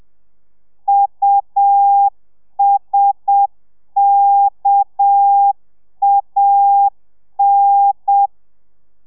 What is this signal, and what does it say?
Morse code, heard as a keyed tone, about 7 wpm, 800 Hz. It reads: USKAN